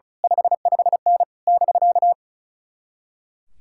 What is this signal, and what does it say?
Morse code, heard as a keyed tone, about 32 wpm, 700 Hz. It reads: 55N <BK>
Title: Speed-Racing Common QSO Phrases - 35wpm